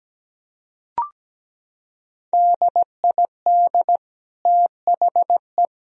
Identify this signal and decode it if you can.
Morse code, heard as a keyed tone, about 17 wpm, 700 Hz. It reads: DID THE
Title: Sentences from Top 1000 Words - Review Part 4 - 17wpm